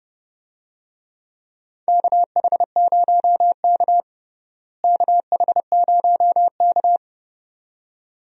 Morse code, heard as a keyed tone, about 30 wpm, 700 Hz. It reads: KH0K KH0K
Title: USA Call Signs - 4 Characters - Repeated 2x - 30wpm